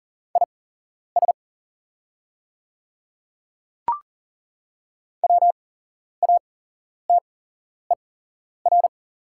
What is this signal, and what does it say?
Morse code, heard as a keyed tone, about 41 wpm, spaced as for 11 wpm, 700 Hz. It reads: IS WATER
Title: 8x Character Spacing - Top 300 Words - 40wpm